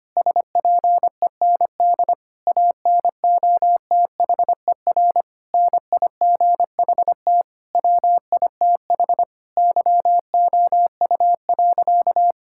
Morse code, heard as a keyed tone, about 25 wpm, 700 Hz. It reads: SPEND ANOTHER NIGHT WITH YOU.
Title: Sentences from Top 1000 Words - Review Part 5 - 25wpm